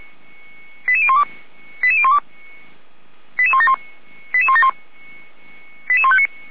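Five-tone selective call (ZVEI), heard as quick runs of five tones, tones of about 70 ms. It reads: 88012 88012 88171 88171
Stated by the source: ZVEI five-tone call received by SDR.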